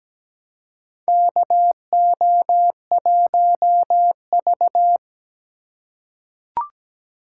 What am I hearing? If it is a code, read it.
Morse code, heard as a keyed tone, about 17 wpm, 700 Hz. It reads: KO1V
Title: USA Call Signs - 4x Character Spacing - 17wpm